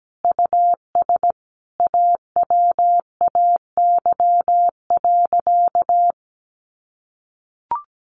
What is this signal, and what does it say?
Morse code, heard as a keyed tone, about 17 wpm, 700 Hz. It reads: US AWAY.
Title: Sentences from Top 1500 Words 17wpm